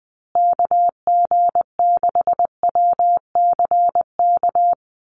Morse code, heard as a keyed tone, about 20 wpm, 700 Hz. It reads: KG6WCK
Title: USA Call Signs Repeated 3x 20wpm